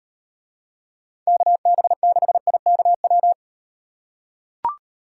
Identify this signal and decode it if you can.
Morse code, heard as a keyed tone, about 37 wpm, 700 Hz. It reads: KB6IKW